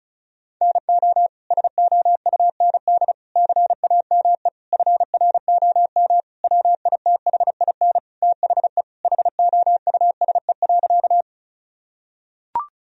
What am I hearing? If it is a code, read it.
Morse code, heard as a keyed tone, about 36 wpm, 700 Hz. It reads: NO SOUND CAME FROM WITHIN THE HOUSE.
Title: Sentences from Top 1000 Words - Review Part 4 - 35wpm